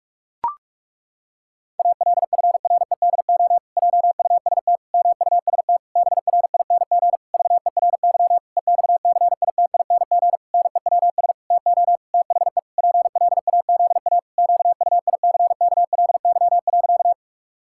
Morse code, heard as a keyed tone, about 46 wpm, 700 Hz. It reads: ALFREDO JUST MUST BRING VERY EXCITING NEWS TO THE PLAZA QUICKLY.
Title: Pangram Challenge v2 - 45wpm